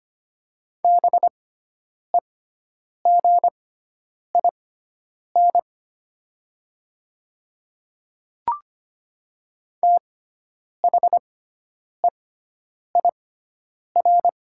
Morse code, heard as a keyed tone, about 25 wpm, spaced as for 9 wpm, 700 Hz. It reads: BEGIN THEIR
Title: 6x Character Spacing - Top 500 Words - 25wpm